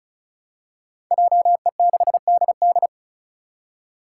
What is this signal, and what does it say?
Morse code, heard as a keyed tone, about 36 wpm, 700 Hz. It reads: JE6DD
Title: Five Challenging Letter-Number Combinations 35wpm